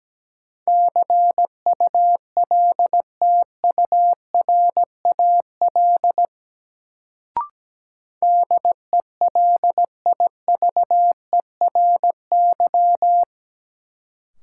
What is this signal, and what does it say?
Morse code, heard as a keyed tone, about 17 wpm, 700 Hz. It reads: CULTURAL DELIVERY
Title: Sentences from Top 1500 Words - New Word Drill Alt - 17wpm